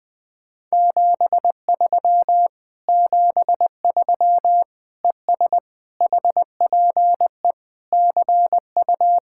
Morse code, heard as a keyed tone, about 20 wpm, 700 Hz. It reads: 73 73 ES HPE CU
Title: Speed-Racing Common QSO Phrases - 20wpm